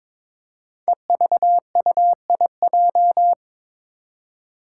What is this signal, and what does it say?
Morse code, heard as a keyed tone, about 21 wpm, 700 Hz. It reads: EVUIJ
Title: Five Challenging Letter-Number Combinations 22wpm